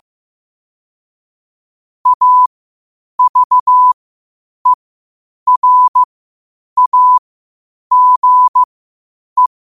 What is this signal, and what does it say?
Morse code, heard as a keyed tone, about 15 wpm, spaced as for 8 wpm, 1000 Hz. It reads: AVERAGE
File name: qotdfiles / 8wpm